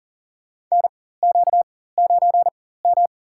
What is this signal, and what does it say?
Morse code, heard as a keyed tone, about 40 wpm, spaced as for 19 wpm, 700 Hz. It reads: NQ9M
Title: USA Call Signs - 4 Characters - 4x Character Spacing - 40wpm